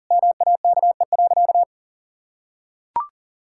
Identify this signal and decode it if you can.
Morse code, heard as a keyed tone, about 40 wpm, 700 Hz. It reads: MAKE.
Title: Sentences from Top 1000 Words - Review Part 5 - 40wpm